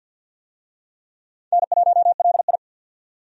Morse code, heard as a keyed tone, about 50 wpm, 700 Hz. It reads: N1LI
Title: USA Call Signs - 4 Characters - 50wpm